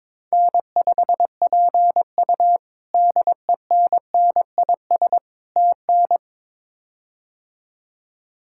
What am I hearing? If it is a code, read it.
Morse code, heard as a keyed tone, about 22 wpm, 700 Hz. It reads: N5PU DENNIS TN
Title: K1USN SST QSO Practice - 22wpm